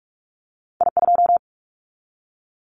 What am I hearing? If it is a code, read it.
Morse code, heard as a keyed tone, about 46 wpm, 700 Hz. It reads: I2
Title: Callsign Prefixes for Top 50 DXCC - 45wpm